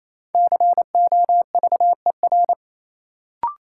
Morse code, heard as a keyed tone, about 28 wpm, 700 Hz. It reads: COVER
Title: Sentences from Top 500 Words - New Word Drill - 28wpm